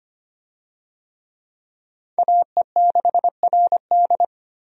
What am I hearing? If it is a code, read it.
Morse code, heard as a keyed tone, about 25 wpm, 700 Hz. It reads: AE6RD
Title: USA Call Signs - 5 Characters - 25wpm